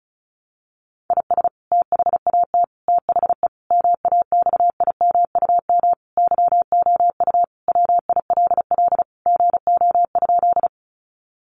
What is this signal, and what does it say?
Morse code, heard as a keyed tone, about 34 wpm, 700 Hz. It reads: IS THAT THE MAXIMUM YOU WILL GO?
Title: Sentences from Top 1300 Words 35wpm